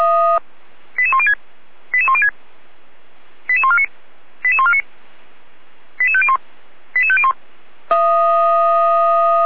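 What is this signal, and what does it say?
Five-tone selective call (ZVEI), heard as quick runs of five tones, tones of about 70 ms. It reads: 88106 88106 88149 88149 88501 88501
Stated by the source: ZVEI five-tone selective call received with an SDR.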